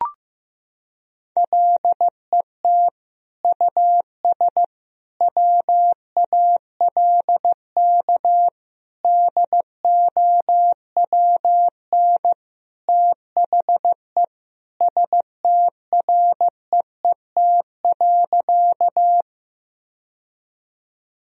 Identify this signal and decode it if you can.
Morse code, heard as a keyed tone, about 15 wpm, 700 Hz. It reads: LET US WALK DOWN THE STREET.
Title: Sentences from Top 1000 Words - Review Part 3 - 15wpm